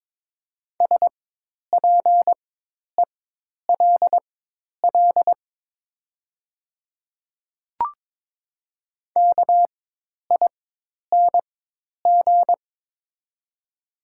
Morse code, heard as a keyed tone, about 22 wpm, spaced as for 10 wpm, 700 Hz. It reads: SPELL KING
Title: 4x Character Spacing - Top 500 Words - 22wpm